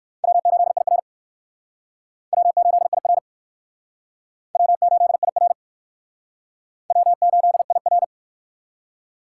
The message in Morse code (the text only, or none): W8IR W8IR W8IR W8IR